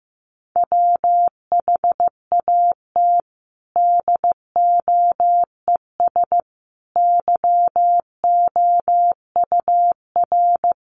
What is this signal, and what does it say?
Morse code, heard as a keyed tone, about 15 wpm, 700 Hz. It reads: WHAT DOES YOUR
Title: Sentences from Top 1400 Words 15wpm